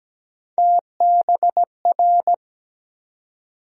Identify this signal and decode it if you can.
Morse code, heard as a keyed tone, about 17 wpm, 700 Hz. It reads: TBR
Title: USA Call Sign Postfixes - 3 Letters 17wpm